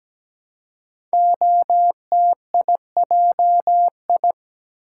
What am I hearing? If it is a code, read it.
Morse code, heard as a keyed tone, about 17 wpm, 700 Hz. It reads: OTIJI
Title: Code Groups - 5 Letters - 17wpm